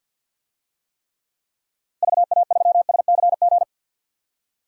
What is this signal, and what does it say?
Morse code, heard as a keyed tone, about 47 wpm, 700 Hz. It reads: VA3SCG
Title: DXCC-100 Call Signs - 6 Characters - Repeated 2x - 50wpm